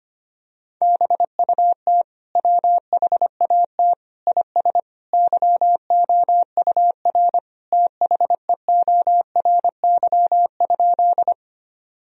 Morse code, heard as a keyed tone, about 25 wpm, 700 Hz. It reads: BUT WHAT IS YOUR THEORY?
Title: Sentences from Top 1000 Words - Review Part 3 - 25wpm